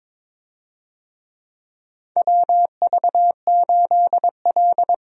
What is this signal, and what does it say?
Morse code, heard as a keyed tone, about 22 wpm, 700 Hz. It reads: WV8L
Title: USA Call Signs - 4 Characters - Repeated 2x - 22wpm